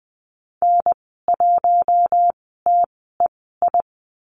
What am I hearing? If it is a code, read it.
Morse code, heard as a keyed tone, about 20 wpm, spaced as for 15 wpm, 700 Hz. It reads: N1TEI
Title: 042 Lesson 8g - Number 1 - Call Signs 20wpm